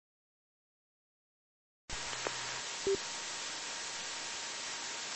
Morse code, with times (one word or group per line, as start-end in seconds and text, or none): none